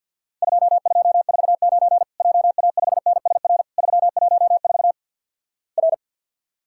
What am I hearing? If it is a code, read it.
Morse code, heard as a keyed tone, about 52 wpm, 700 Hz. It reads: 2249 JA5NSR 314 R
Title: Worked All Europe DX Contest - QTC Exchange with Repeats - 50wpm